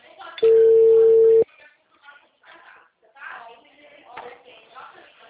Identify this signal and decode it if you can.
Ringback tone (China, 450 Hz), with speech.
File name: out- -2022- - - .7586